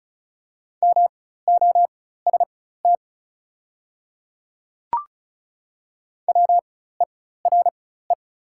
Morse code, heard as a keyed tone, about 36 wpm, spaced as for 16 wpm, 700 Hz. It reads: MOST WERE
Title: 4x Character Spacing - Top 100 Words - 35wpm